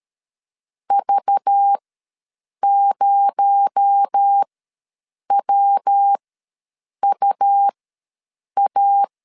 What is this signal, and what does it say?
Morse code, heard as a keyed tone, about 13 wpm, spaced as for 7 wpm, 800 Hz. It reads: V0WUA
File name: LEZIONI / 7 GF9Z / 7 G ws 12wpm 3140